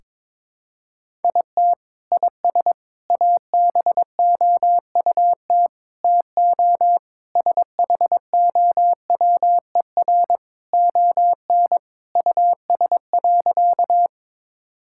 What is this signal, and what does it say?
Morse code, heard as a keyed tone, about 22 wpm, 700 Hz. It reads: IT IS ABOUT TO SHOWER ON US.
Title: Sentences from Top 1600 Words 22wpm